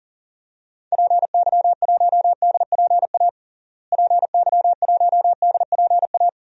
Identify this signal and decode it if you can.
Morse code, heard as a keyed tone, about 40 wpm, 700 Hz. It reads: PY1DPA PY1DPA
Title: DXCC-100 Call Signs - 6 Characters - Repeated 2x - 40wpm